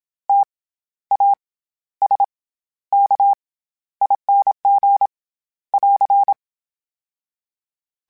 Morse code, heard as a keyed tone, about 27 wpm, 800 Hz. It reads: T A S K ING <AR>
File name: Special INGs-mp3s / Special INGs-mp3s / SlowToFaster ing @25